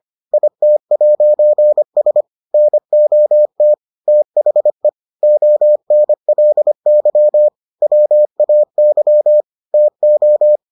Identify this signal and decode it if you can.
Morse code, heard as a keyed tone, about 25 wpm, 600 Hz. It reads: IT'S NOT THE ONLY WAY TO